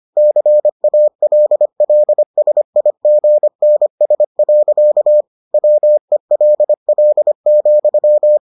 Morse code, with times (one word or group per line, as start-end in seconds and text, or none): 0.2-5.2 CALLSIGNS.
5.5-8.4 WELL,